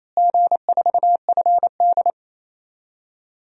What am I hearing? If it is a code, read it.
Morse code, heard as a keyed tone, about 28 wpm, 700 Hz. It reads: G4FD
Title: DXCC-100 Call Signs - 4 Characters - Repeated 3x - 28wpm